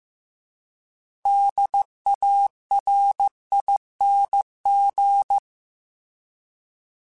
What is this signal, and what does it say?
Morse code, heard as a keyed tone, about 15 wpm, 800 Hz. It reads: DARING